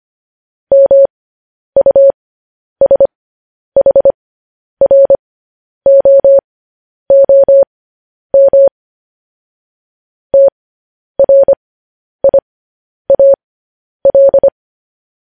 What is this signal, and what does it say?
Morse code, heard as a keyed tone, about 25 wpm, spaced as for 10 wpm, 550 Hz. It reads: MUSHROOM TRIAL
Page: news-headlines-in-morse-code-at-10-wpm